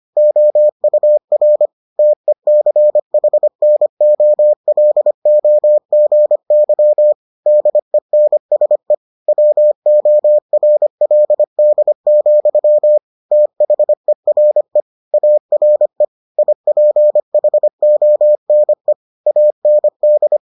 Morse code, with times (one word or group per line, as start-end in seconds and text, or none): 0.2-1.7 OUR
2.0-7.1 TECHNOLOGY
7.5-8.9 DENSE
9.3-13.0 WORLD,
13.3-14.8 THERE
15.1-16.0 ARE
16.4-18.9 IPHONE
19.3-20.4 AND